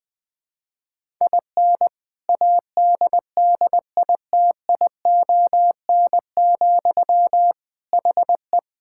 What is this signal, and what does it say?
Morse code, heard as a keyed tone, about 19 wpm, 700 Hz. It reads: IN ADDITION, HE
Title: Sentences from Top 1000 Words - Review Part 6 - 20wpm